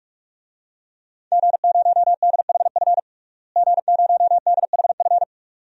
Morse code, heard as a keyed tone, about 41 wpm, 700 Hz. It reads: G0DHF G0DHF
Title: DXCC-100 Call Signs - 5 Characters - Repeated 2x - 45wpm